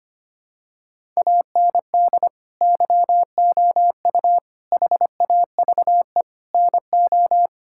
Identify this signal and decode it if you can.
Morse code, heard as a keyed tone, about 25 wpm, 700 Hz. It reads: AND YOU HAVE NO
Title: Sentences from Top 1000 Words - Review Part 6 - 25wpm